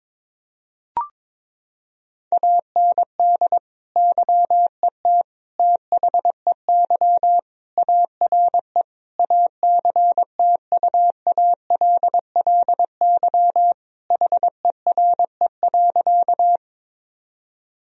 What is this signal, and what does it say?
Morse code, heard as a keyed tone, about 22 wpm, 700 Hz. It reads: AND YET THEY ARE ACTUALLY HERE.
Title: Sentences from Top 1000 Words - Review Part 4 - 22wpm